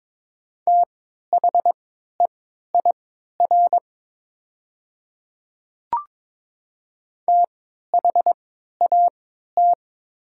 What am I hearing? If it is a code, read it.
Morse code, heard as a keyed tone, about 22 wpm, spaced as for 13 wpm, 700 Hz. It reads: THEIR THAT
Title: 3x Character Spacing - Top 100 Words - 22wpm